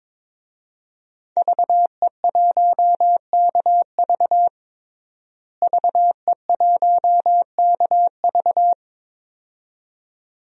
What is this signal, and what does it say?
Morse code, heard as a keyed tone, about 22 wpm, 700 Hz. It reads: VE1KV VE1KV
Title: DXCC-100 Call Signs - 5 Characters - Repeated 2x - 22wpm